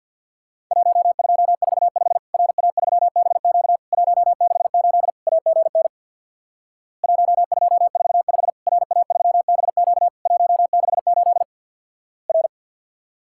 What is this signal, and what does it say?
Morse code, heard as a keyed tone, about 50 wpm, 700 Hz. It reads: 1245 RA3BQ 168 AGN 1245 RA3BQ 168 R
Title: Worked All Europe DX Contest - QTC Exchange with Repeats - 50wpm